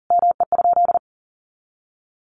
Morse code, heard as a keyed tone, about 40 wpm, 700 Hz. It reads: ME?
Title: Sentences from Top 700 Words 40wpm